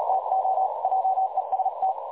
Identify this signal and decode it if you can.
Morse code, heard as a keyed tone, about 37 wpm, 760 Hz, with noise. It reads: TO JEST
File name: dlazaawansowanych / nagraniazpasma / websdr recording 3565.0kHz